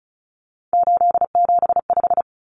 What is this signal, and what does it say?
Morse code, heard as a keyed tone, about 34 wpm, 700 Hz. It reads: 875
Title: Numbers - 3-Digit 35wpm